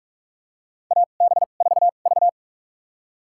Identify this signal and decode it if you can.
Morse code, heard as a keyed tone, about 46 wpm, spaced as for 33 wpm, 690 Hz. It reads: AB4V
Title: USA Call Signs - 4 Characters - 2x Character Spacing - 45wpm